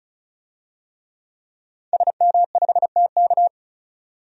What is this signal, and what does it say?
Morse code, heard as a keyed tone, about 34 wpm, 700 Hz. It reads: SM5TK
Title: DXCC-100 Call Signs - 5 Characters - 35wpm